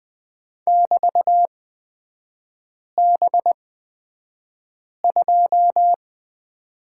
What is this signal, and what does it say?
Morse code, heard as a keyed tone, about 20 wpm, spaced as for 5 wpm, 700 Hz. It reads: <BT>B2
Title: 250 Lesson 40e - BK - Three Characters 20wpm